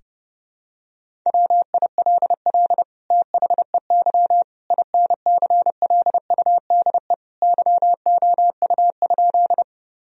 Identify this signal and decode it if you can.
Morse code, heard as a keyed tone, about 32 wpm, 700 Hz. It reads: WILL THEY INCLUDE YOU?
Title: Sentences from Top 1000 Words - Review Part 3 - 30wpm